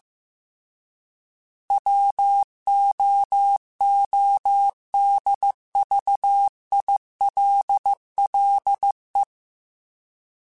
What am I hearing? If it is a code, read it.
Morse code, heard as a keyed tone, about 15 wpm, 800 Hz. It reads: WOODVILLE